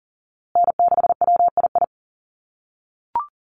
Morse code, heard as a keyed tone, about 38 wpm, 700 Hz. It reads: N6WII
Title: USA Call Signs Repeated 3x 40wpm